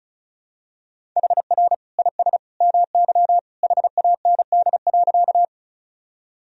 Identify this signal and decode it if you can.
Morse code, heard as a keyed tone, about 34 wpm, 700 Hz. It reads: HR IS MY HAND.
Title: Sentences from Top 200 Words and Abbreviations 35wpm